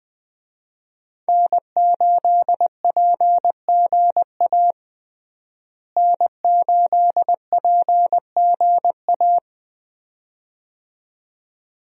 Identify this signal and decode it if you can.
Morse code, heard as a keyed tone, about 21 wpm, 700 Hz. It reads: N8PGA N8PGA
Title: USA Call Signs - 5 Characters - Repeated 2x - 20wpm